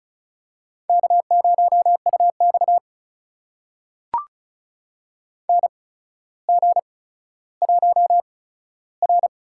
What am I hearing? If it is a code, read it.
Morse code, heard as a keyed tone, about 36 wpm, spaced as for 15 wpm, 700 Hz. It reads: K0UX NG1R
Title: USA Call Signs - 4 Characters - 8x Character Spacing - 35wpm